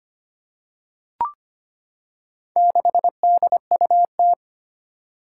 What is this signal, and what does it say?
Morse code, heard as a keyed tone, about 25 wpm, 700 Hz. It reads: 6DUT
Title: Four Challenging Letter-Number Combinations 25wpm